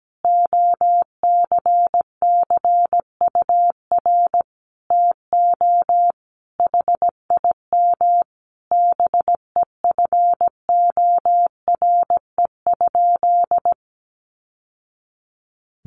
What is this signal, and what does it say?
Morse code, heard as a keyed tone, about 17 wpm, 700 Hz. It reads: OCCUR TO HIM BEFORE?
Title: Sentences from Top 900 Words 17wpm